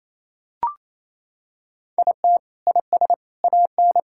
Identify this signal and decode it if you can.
Morse code, heard as a keyed tone, about 28 wpm, 700 Hz. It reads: IT IS AN
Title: Sentences from Top 1000 Words - Review Part 6 - 28wpm